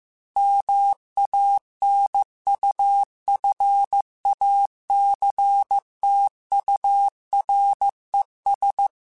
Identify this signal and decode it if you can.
Morse code, heard as a keyed tone, about 15 wpm, 800 Hz. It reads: MANUFACTURES
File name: Chuckles / Chuckles-mp3s / Guaranteed @13